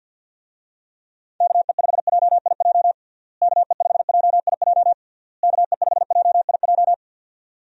Morse code, heard as a keyed tone, about 48 wpm, 700 Hz. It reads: KE5JIJ KE5JIJ KE5JIJ
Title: USA Call Signs - 6 Characters - Repeated 3x - 50wpm